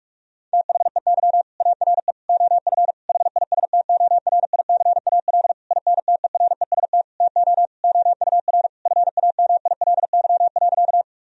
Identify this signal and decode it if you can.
Morse code, heard as a keyed tone, about 46 wpm, 700 Hz. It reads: THEY ARE OF HISTORICAL INTEREST TO OUR FAMILY.
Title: Sentences from Top 1000 Words - Review Part 4 - 45wpm